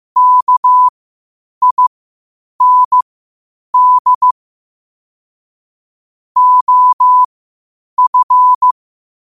Morse code, heard as a keyed tone, about 15 wpm, spaced as for 8 wpm, 1000 Hz. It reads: KIND OF
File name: qotdfiles / 8wpm